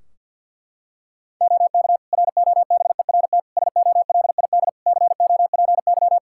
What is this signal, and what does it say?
Morse code, heard as a keyed tone, about 52 wpm, 700 Hz. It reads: OK ROBERT SOLID COPY
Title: K1USN SST QSO Practice 50wpm